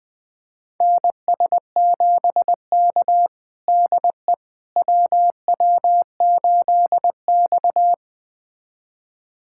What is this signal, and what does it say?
Morse code, heard as a keyed tone, about 20 wpm, 700 Hz. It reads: NS7K DE WW8X
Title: USA Callsign DE Callsign - 4 Characters - 20wpm